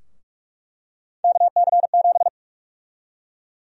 Morse code, heard as a keyed tone, about 46 wpm, 700 Hz. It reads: KC7
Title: USA Call Sign Prefixes - Repeated 2x - 45wpm